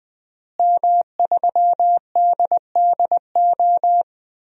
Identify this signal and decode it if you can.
Morse code, heard as a keyed tone, about 20 wpm, 700 Hz. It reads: M3DDO
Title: DXCC-100 Call Signs - 5 Characters - Repeated 3x - 20wpm